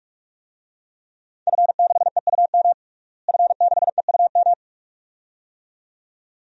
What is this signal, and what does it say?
Morse code, heard as a keyed tone, about 42 wpm, 700 Hz. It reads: F6EUM F6EUM
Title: DXCC-100 Call Signs - 5 Characters - Repeated 2x - 45wpm